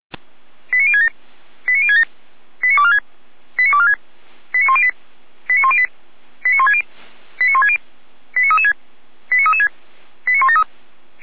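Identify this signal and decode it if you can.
Five-tone selective call (ZVEI), heard as quick runs of five tones, tones of about 70 ms. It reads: 89067 89067 89246 89246 89108 89108 89160 89160 89336 89336 89173
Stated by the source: ZVEI five-tone call 89099 received by SDR, from the audio server of the Feuerwehr Orlamünde fire brigade.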